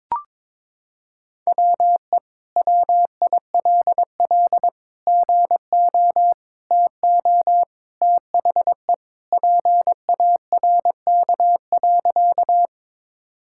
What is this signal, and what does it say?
Morse code, heard as a keyed tone, about 22 wpm, 700 Hz. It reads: WE WILL GO TO THE PARK.
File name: practice / 22wpm / Sentences with 2-Word Phrases - 22wpm